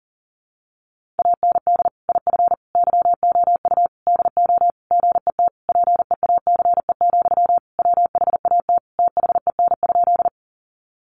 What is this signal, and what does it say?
Morse code, heard as a keyed tone, about 40 wpm, 700 Hz. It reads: AND IF YOU DO GET PEACE, WHAT THEN?
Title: Sentences from Top 1500 Words 40wpm